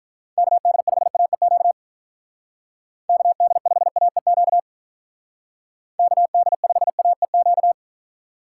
Morse code, heard as a keyed tone, about 46 wpm, 700 Hz. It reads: KD5AEQ KD5AEQ KD5AEQ